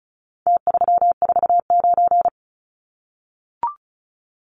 Morse code, heard as a keyed tone, about 34 wpm, 700 Hz. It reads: T349
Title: Numbers - Contest Style - 4-Digit 35wpm